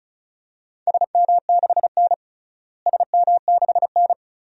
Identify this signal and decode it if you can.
Morse code, heard as a keyed tone, about 34 wpm, 700 Hz. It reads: SM6N SM6N
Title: DXCC-100 Call Signs - 4 Characters - Repeated 2x - 35wpm